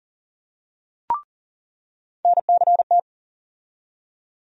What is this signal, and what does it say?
Morse code, heard as a keyed tone, about 40 wpm, 700 Hz. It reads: NCT
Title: USA Call Sign Postfixes - 3 Letters 40wpm